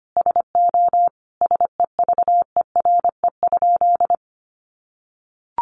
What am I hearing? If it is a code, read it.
Morse code, heard as a keyed tone, about 25 wpm, 700 Hz. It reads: SO SEVERE?
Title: Sentences from Top 1400 Words 25wpm